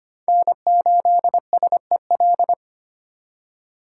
Morse code, heard as a keyed tone, about 25 wpm, 700 Hz. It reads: N8SEL
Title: USA Call Signs - 5 Characters - 2x Character Spacing - 25wpm